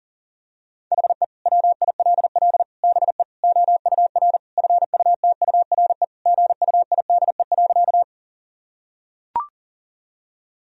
Morse code, heard as a keyed tone, about 40 wpm, 700 Hz. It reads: HE WILL BE OUR FUTURE GUIDE.
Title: Sentences from Top 1000 Words - Review Part 3 - 40wpm